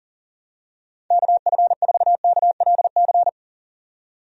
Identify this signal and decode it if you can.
Morse code, heard as a keyed tone, about 40 wpm, 700 Hz. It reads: KF4KLC
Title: USA Call Signs - 6 Characters - 40wpm